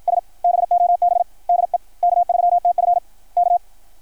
Morse code, heard as a keyed tone, about 53 wpm, 690 Hz. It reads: S BOB DE K3TF K
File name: archived / CRSnet